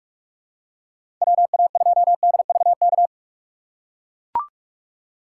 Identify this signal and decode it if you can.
Morse code, heard as a keyed tone, about 42 wpm, 700 Hz. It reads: WA2DVK
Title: USA Call Signs - 6 Characters - 3x Character Spacing - 45wpm